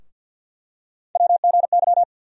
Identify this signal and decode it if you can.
Morse code, heard as a keyed tone, about 51 wpm, 700 Hz. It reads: WGY
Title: USA Call Sign Postfixes - 3 Letters 50wpm